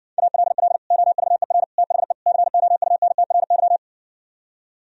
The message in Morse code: ALL OVER THE COUNTRY

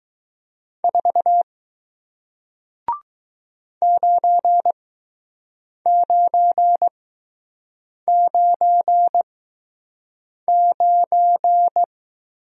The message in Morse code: 4 9999

